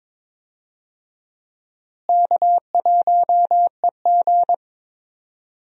Morse code, heard as a keyed tone, about 22 wpm, 700 Hz. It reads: K1EG
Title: USA Call Signs - 4 Characters - Repeated 3x - 22wpm